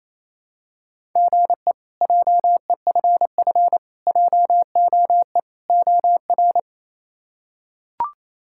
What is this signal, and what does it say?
Morse code, heard as a keyed tone, about 28 wpm, 700 Hz. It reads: GE JEFF JOE OR
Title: K1USN SST QSO Practice - 28wpm